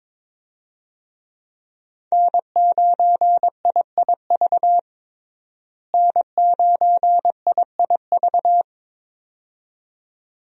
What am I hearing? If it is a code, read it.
Morse code, heard as a keyed tone, about 21 wpm, 700 Hz. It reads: N9IIV N9IIV
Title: USA Call Signs - 5 Characters - Repeated 2x - 22wpm